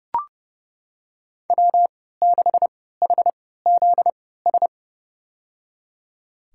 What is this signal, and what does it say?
Morse code, heard as a keyed tone, about 30 wpm, spaced as for 17 wpm, 700 Hz. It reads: W6HZS
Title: USA Call Signs - 5 Characters - 3x Character Spacing - 30wpm